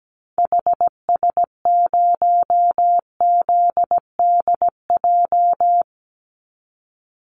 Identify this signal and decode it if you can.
Morse code, heard as a keyed tone, about 17 wpm, 700 Hz. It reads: HS0ZDJ
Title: Call Signs - 17wpm